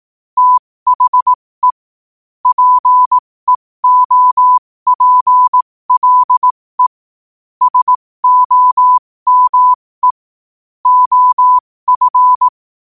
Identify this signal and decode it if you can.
Morse code, heard as a keyed tone, about 18 wpm, spaced as for 15 wpm, 1000 Hz. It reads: THE PEOPLE SOME OF
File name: 15wpm